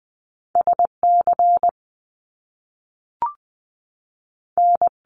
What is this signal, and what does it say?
Morse code, heard as a keyed tone, about 20 wpm, 700 Hz. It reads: S C N
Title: Top Two Letter Combinations in Words 20wpm